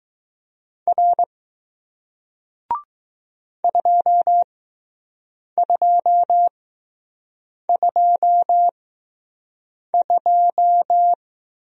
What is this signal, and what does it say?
Morse code, heard as a keyed tone, about 20 wpm, spaced as for 6 wpm, 700 Hz. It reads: R 2222